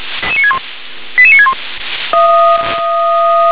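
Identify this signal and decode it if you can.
Five-tone selective call (ZVEI), heard as quick runs of five tones, tones of about 70 ms. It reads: 88961
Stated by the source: ZVEI five-tone selective call received by SDR.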